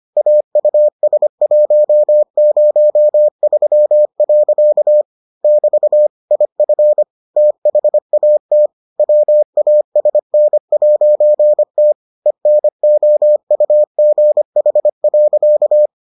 Morse code, heard as a keyed tone, about 25 wpm, 600 Hz. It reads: AUS103. <BT> IF THAT WASN'T ENOUGH.